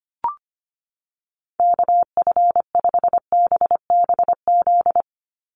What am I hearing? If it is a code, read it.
Morse code, heard as a keyed tone, about 25 wpm, 700 Hz. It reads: KF5BBZ
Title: USA Call Signs 25wpm